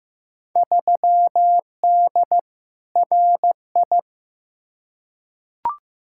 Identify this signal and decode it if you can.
Morse code, heard as a keyed tone, about 15 wpm, 700 Hz. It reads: 3D RI
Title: Field Day Exchanges Encore - 15wpm